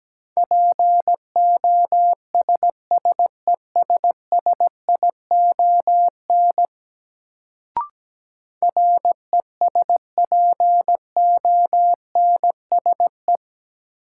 Morse code, heard as a keyed tone, about 17 wpm, 700 Hz. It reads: POSSESSION RESPONSE
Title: Sentences from Top 1400 Words - New Word Drill Alt - 17wpm